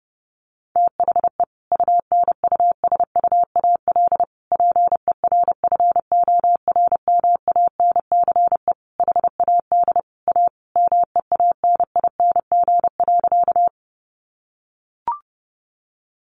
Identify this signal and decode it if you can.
Morse code, heard as a keyed tone, about 30 wpm, 700 Hz. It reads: THE UNUSUAL PERFORMANCE HAD A MEANING.
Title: Sentences from Top 1400 Words 30wpm